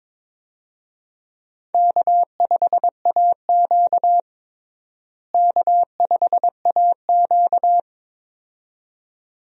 Morse code, heard as a keyed tone, about 22 wpm, 700 Hz. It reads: K5AQ K5AQ
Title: USA Call Signs - 4 Characters - Repeated 2x - 22wpm